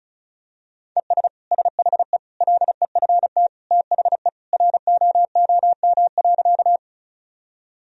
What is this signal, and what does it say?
Morse code, heard as a keyed tone, about 34 wpm, 700 Hz. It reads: ES SHE LEFT THE ROOM.
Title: Sentences from Top 300 Words and Abbreviations 35wpm